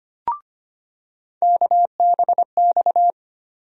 Morse code, heard as a keyed tone, about 25 wpm, 700 Hz. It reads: KBX
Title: USA Call Sign Postfixes - 3 Letters 25wpm